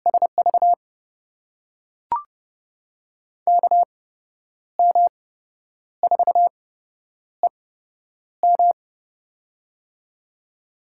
Morse code, heard as a keyed tone, about 30 wpm, spaced as for 10 wpm, 700 Hz. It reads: SV KM4EM
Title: USA Call Signs - 5 Characters - 8x Character Spacing - 30wpm